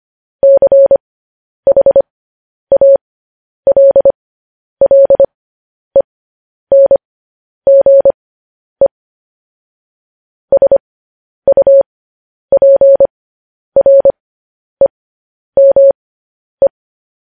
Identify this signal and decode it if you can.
Morse code, heard as a keyed tone, about 25 wpm, spaced as for 10 wpm, 550 Hz. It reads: CHALLENGE SUPREME
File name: podcast / News headlines-1x-25x10